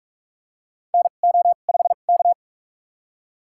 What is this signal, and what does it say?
Morse code, heard as a keyed tone, about 46 wpm, spaced as for 32 wpm, 700 Hz. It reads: NO5K
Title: USA Call Signs - 4 Characters - 2x Character Spacing - 45wpm